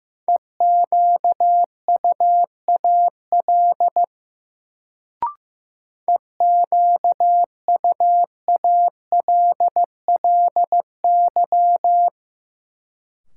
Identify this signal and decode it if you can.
Morse code, heard as a keyed tone, about 16 wpm, 700 Hz. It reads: EQUAL EQUALLY
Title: Sentences from Top 1400 Words - New Word Drill Alt - 15wpm